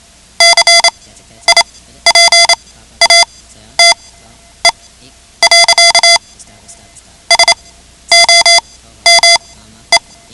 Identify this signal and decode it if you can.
Morse code, heard as a keyed tone, about 28 wpm, spaced as for 12 wpm, 2050 Hz. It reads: CIPATE. SOME
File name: ham radio / CodeQuick / Code Quick 10